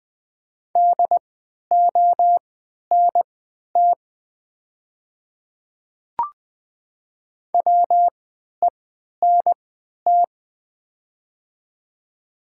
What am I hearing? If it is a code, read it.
Morse code, heard as a keyed tone, about 20 wpm, spaced as for 11 wpm, 700 Hz. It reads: DONT WENT
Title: 3x Character Spacing - Top 300 Words - 20wpm